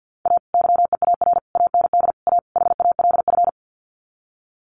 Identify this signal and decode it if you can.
Morse code, heard as a keyed tone, about 50 wpm, 700 Hz. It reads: A YEAR AND A HALF
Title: Sets of 5 Words 50wpm